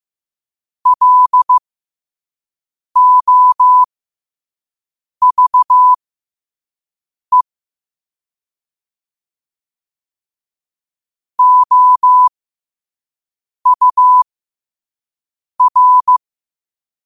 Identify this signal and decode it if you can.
Morse code, heard as a keyed tone, about 15 wpm, spaced as for 5 wpm, 1000 Hz. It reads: LOVE OUR
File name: qotdfiles / 5wpm